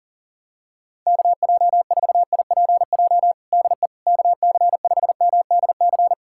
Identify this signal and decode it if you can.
Morse code, heard as a keyed tone, about 40 wpm, 700 Hz. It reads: KJ4IPJ DE KC5MDC
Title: USA Callsign DE Callsign - 6 Characters - 40wpm